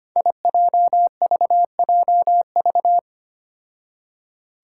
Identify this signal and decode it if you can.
Morse code, heard as a keyed tone, about 25 wpm, 700 Hz. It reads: IJVJV
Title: Five Challenging Letter-Number Combinations 25wpm